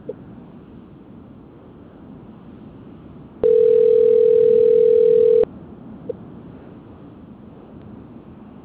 Ringback tone (North America), 440 + 480 Hz.
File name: out- - - - - .1291